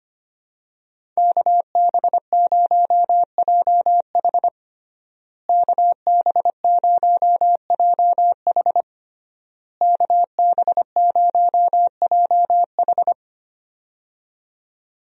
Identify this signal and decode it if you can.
Morse code, heard as a keyed tone, about 25 wpm, 700 Hz. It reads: KB0JH KB0JH KB0JH
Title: USA Call Signs - 5 Characters - Repeated 3x - 25wpm